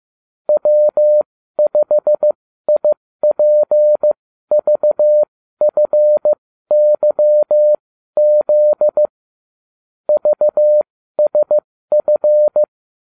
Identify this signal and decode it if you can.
Morse code, heard as a keyed tone, about 15 wpm, spaced as for 12 wpm, 600 Hz. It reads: W5IPVFYZ VSF